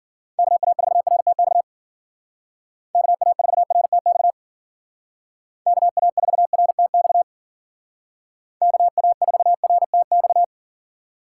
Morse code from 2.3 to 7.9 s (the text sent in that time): KA4RTX KA4RTX